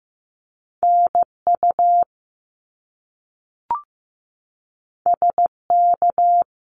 Morse code, heard as a keyed tone, about 15 wpm, 700 Hz. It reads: NU SK
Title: Canadian Provinces and Territories - Repeated 3x - 15wpm